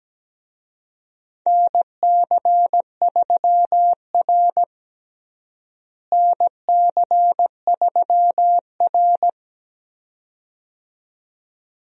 Morse code, heard as a keyed tone, about 17 wpm, 700 Hz. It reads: NC3R NC3R
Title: USA Call Signs - 4 Characters - Repeated 2x - 17wpm